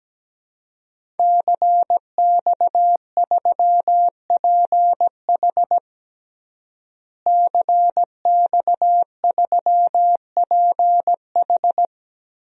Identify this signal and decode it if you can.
Morse code, heard as a keyed tone, about 17 wpm, 700 Hz. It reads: CX3PH CX3PH
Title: DXCC-100 Call Signs - 5 Characters - Repeated 3x - 17wpm